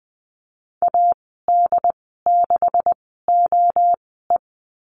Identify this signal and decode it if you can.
Morse code, heard as a keyed tone, about 20 wpm, spaced as for 15 wpm, 700 Hz. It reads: AD6OE
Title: 161 Lesson 26g - Letter g - Call Signs 20wpm